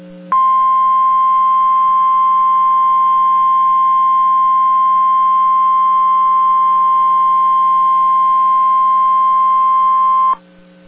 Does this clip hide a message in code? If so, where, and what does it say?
Yes: Emergency Alert System (SAME) tones (the 1050 Hz attention tone) from 0.3 to 10.4 s.